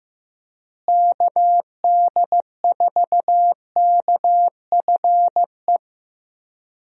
Morse code, heard as a keyed tone, about 15 wpm, 700 Hz. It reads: KD4KFE
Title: USA Call Signs - 6 Characters - Repeated 3x - 15wpm